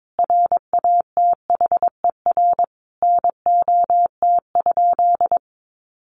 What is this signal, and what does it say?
Morse code, heard as a keyed tone, about 22 wpm, 700 Hz. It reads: RATHER NOT?
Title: Sentences from Top 700 Words 22wpm